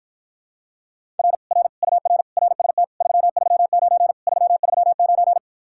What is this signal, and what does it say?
Morse code, heard as a keyed tone, about 54 wpm, 700 Hz. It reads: R R UR RST 339 339